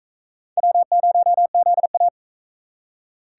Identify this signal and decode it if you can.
Morse code, heard as a keyed tone, about 36 wpm, 700 Hz. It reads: W0ZA